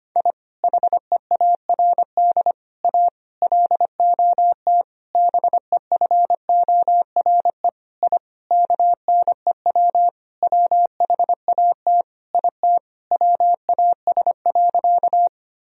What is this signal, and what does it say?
Morse code, heard as a keyed tone, about 25 wpm, 700 Hz. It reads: I HEARD A LOT BEFORE I KNEW WHAT IT WAS.
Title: Sentences from Top 1000 Words - Review Part 3 - 25wpm